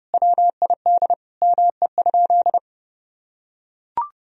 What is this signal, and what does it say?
Morse code, heard as a keyed tone, about 30 wpm, 700 Hz. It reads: WID ME?
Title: Sentences from Top 200 Words and Abbreviations 30wpm